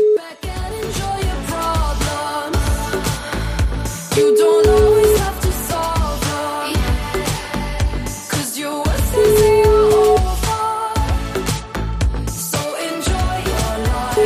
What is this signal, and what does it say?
Ringback tone (Europe, 425 Hz), with noise.